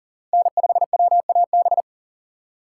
Morse code, heard as a keyed tone, about 39 wpm, 700 Hz. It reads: N5WAB
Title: USA Call Signs - 5 Characters - 3x Character Spacing - 40wpm